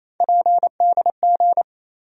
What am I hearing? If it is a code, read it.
Morse code, heard as a keyed tone, about 28 wpm, 700 Hz. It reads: PDG
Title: USA Call Sign Postfixes - 3 Letters 28wpm